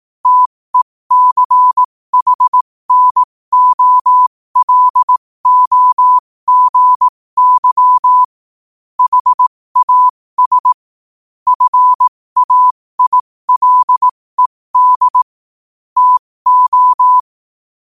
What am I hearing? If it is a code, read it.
Morse code, heard as a keyed tone, about 18 wpm, spaced as for 15 wpm, 1000 Hz. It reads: TECHNOLOGY HAS FAILED TO
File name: qotdfiles / 15wpm